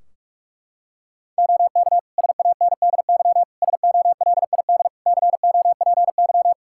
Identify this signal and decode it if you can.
Morse code, heard as a keyed tone, about 46 wpm, 700 Hz. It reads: OK SANDY SOLID COPY
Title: Practice QSO - Exchange 3 - More Info - 45wpm